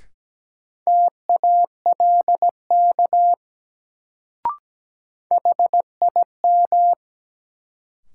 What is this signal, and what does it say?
Morse code, heard as a keyed tone, about 17 wpm, 700 Hz. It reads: TALK HIM
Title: Sets of 2 Words - New Word Drill - 17wpm